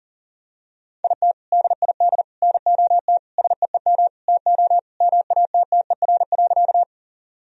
Morse code, heard as a keyed tone, about 40 wpm, 700 Hz. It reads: IT DID NOT SEEM TO MATTER.
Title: Sentences from Top 1000 Words - Review Part 1 - 40wpm